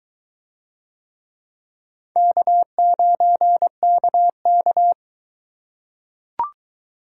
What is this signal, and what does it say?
Morse code, heard as a keyed tone, about 22 wpm, 700 Hz. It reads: K9KK